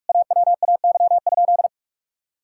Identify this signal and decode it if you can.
Morse code, heard as a keyed tone, about 41 wpm, 700 Hz. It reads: AWAY?
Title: Sentences from Top 1000 Words - Review Part 6 - 45wpm